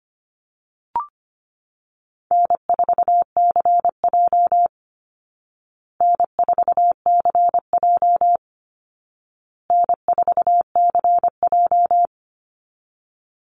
Morse code, heard as a keyed tone, about 25 wpm, 700 Hz. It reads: N4CJ N4CJ N4CJ
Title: Easy Call Signs Repeated 3x - 25wpm